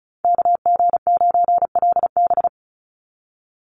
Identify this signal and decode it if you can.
Morse code, heard as a keyed tone, about 34 wpm, 700 Hz. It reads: KG9LB
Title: Call Signs - 35wpm